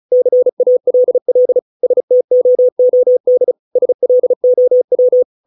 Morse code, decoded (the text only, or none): CALL STOOD SLOW